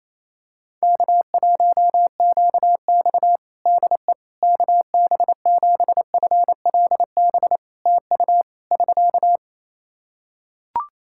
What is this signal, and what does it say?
Morse code, heard as a keyed tone, about 28 wpm, 700 Hz. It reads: K1QX DE KB7FLB TU <SK>
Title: Practice QSO - Exchange 4 - Ending - 28wpm